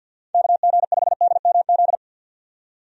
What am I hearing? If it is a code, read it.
Morse code, heard as a keyed tone, about 52 wpm, 700 Hz. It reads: KG5DMZ
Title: USA Call Signs - 6 Characters - 50wpm